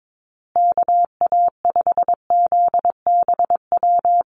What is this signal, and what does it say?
Morse code, heard as a keyed tone, about 22 wpm, 700 Hz. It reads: KA5ZBW
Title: USA Call Signs Repeated 2x 22wpm